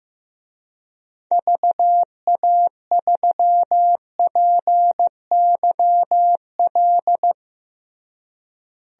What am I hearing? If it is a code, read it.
Morse code, heard as a keyed tone, about 15 wpm, 700 Hz. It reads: VA3PYL
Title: DXCC-100 Call Signs - 6 Characters - Repeated 3x - 15wpm